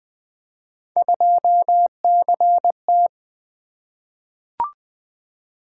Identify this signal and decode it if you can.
Morse code, heard as a keyed tone, about 20 wpm, 700 Hz. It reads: 2CT
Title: Three Letter-Number Combinations - 20wpm